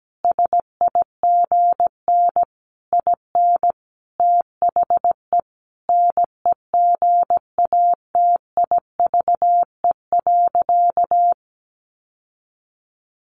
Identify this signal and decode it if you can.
Morse code, heard as a keyed tone, about 17 wpm, 700 Hz. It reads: SIGN IN THE NEGATIVE.
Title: Sentences from Top 1000 Words 17wpm